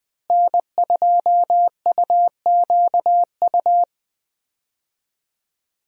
Morse code, heard as a keyed tone, about 21 wpm, 700 Hz. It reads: N2UQU
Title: USA Call Signs - 5 Characters - Repeated 2x - 20wpm